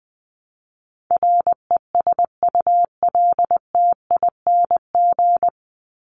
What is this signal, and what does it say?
Morse code, heard as a keyed tone, about 20 wpm, 700 Hz. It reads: RESULTING
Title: ING Suffix - Top 100 Words with Base Words 20wpm